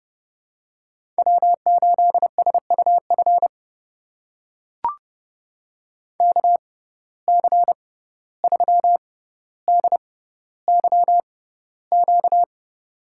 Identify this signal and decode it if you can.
Morse code, heard as a keyed tone, about 30 wpm, spaced as for 15 wpm, 700 Hz. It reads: W8SUF KC3DYQ
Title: USA Call Signs - 6x Character Spacing - 30wpm